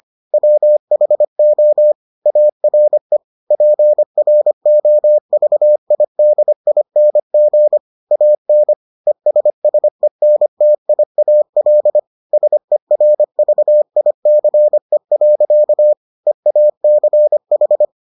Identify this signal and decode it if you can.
Morse code, heard as a keyed tone, about 25 wpm, 600 Hz. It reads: WHO ARE PROVIDING AN ESSENTIAL SERVICE. EACH